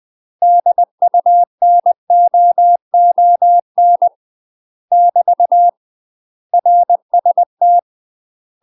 Morse code, heard as a keyed tone, about 20 wpm, 700 Hz. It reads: DUNOON <BT> RST
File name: UK MASTER QSOs TRAINING / 20WPM / UK MASTER QSOs 0067